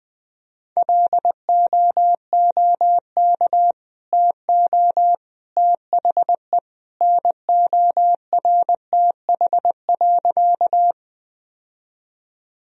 Morse code, heard as a keyed tone, about 21 wpm, 700 Hz. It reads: LOOK TO THE NORTH.
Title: Sentences from Top 1000 Words - Review Part 1 - 20wpm